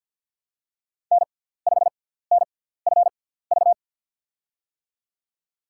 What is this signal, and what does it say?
Morse code, heard as a keyed tone, about 53 wpm, spaced as for 17 wpm, 690 Hz. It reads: N5NFV
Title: USA Call Signs - 5 Characters - 6x Character Spacing - 50wpm